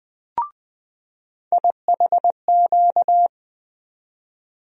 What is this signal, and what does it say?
Morse code, heard as a keyed tone, about 20 wpm, 700 Hz. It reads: IHQ
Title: USA Call Sign Postfixes - 3 Letters 20wpm